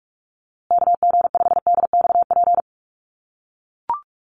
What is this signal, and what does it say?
Morse code, heard as a keyed tone, about 44 wpm, 690 Hz. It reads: KG5DXP
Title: USA Call Signs Repeated 3x 45wpm